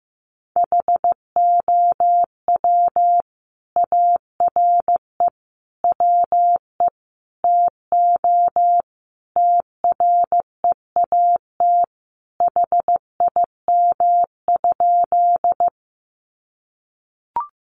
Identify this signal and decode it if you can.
Morse code, heard as a keyed tone, about 15 wpm, 700 Hz. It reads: HOW ARE WE TO TREAT HIM?
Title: Sentences from Top 1300 Words 15wpm